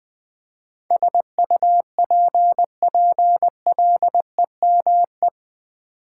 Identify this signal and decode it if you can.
Morse code, heard as a keyed tone, about 20 wpm, 700 Hz. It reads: SUPPLEME
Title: Build A Word - 4 Letters At A Time - 10-Letter Words - 20wpm